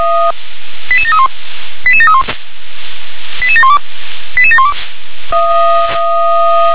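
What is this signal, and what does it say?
Five-tone selective call (ZVEI), heard as quick runs of five tones, tones of about 70 ms. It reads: 88731 88731 88712 88712
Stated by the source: Five-tone ZVEI selective call received by SDR.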